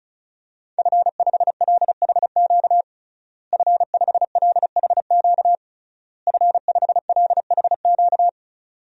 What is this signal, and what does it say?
Morse code, heard as a keyed tone, about 36 wpm, 700 Hz. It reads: F5LHQ F5LHQ F5LHQ
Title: DXCC-100 Call Signs - 5 Characters - Repeated 3x - 35wpm